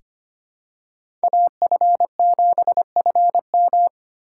Morse code, heard as a keyed tone, about 25 wpm, 700 Hz. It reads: AF7FM
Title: USA Call Signs - 5 Characters - 2x Character Spacing - 25wpm